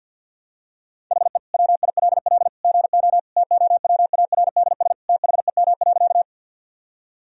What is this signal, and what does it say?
Morse code, heard as a keyed tone, about 50 wpm, 700 Hz. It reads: HE WILL GO TOWARDS THEM.
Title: Sentences from Top 1000 Words - Review Part 2 - 50wpm